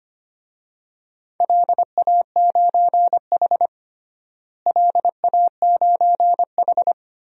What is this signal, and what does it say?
Morse code, heard as a keyed tone, about 25 wpm, 700 Hz. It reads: LA9H LA9H
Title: DXCC-100 Call Signs - 4 Characters - Repeated 2x - 25wpm